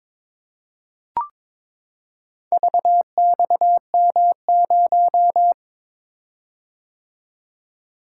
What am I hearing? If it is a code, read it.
Morse code, heard as a keyed tone, about 22 wpm, 700 Hz. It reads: VXM0
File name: practice / 22wpm / Four Letter-Number Combinations - 22wpm